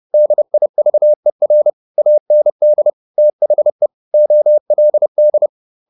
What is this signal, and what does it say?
Morse code, heard as a keyed tone, about 30 wpm, 600 Hz. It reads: DIVER AND THE OLD
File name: the old man and the sea / the old man and the sea 1 30wpm